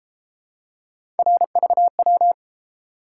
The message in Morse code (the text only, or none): RVW